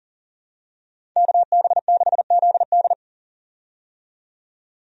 Morse code, heard as a keyed tone, about 40 wpm, 700 Hz. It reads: KB6ZD
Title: USA Call Signs - 5 Characters - 40wpm